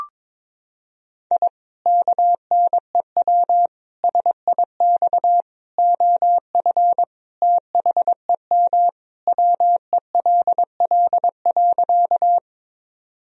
Morse code, heard as a keyed tone, about 22 wpm, 700 Hz. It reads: I KNEW SIX OF THEM WELL.
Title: Sentences from Top 1000 Words - Review Part 2 - 22wpm